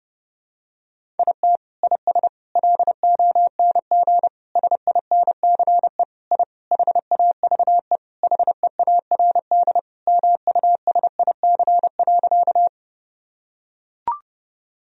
Morse code, heard as a keyed tone, about 30 wpm, 700 Hz. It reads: IT IS LONG SINCE I HAVE HEARD MUSIC.
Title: Sentences from Top 1000 Words - Review Part 1 - 30wpm